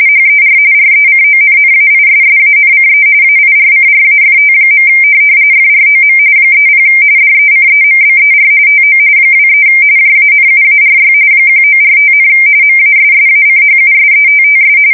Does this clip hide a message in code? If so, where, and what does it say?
Yes: RTTY from 0.1 to 14.8 s, about 45.45 baud. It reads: HER RECONNAISSANCE AIRCRAFT. FROM 30 DECEMBER TO 23 JANUARY 1945, MASSACHUSETTS